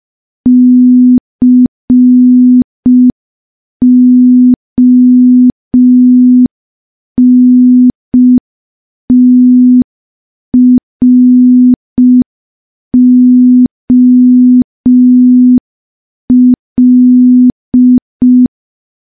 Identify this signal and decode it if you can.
Morse code, heard as a keyed tone, about 5 wpm, 250 Hz. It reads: CONTROL
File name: morse / morse-training-05WPM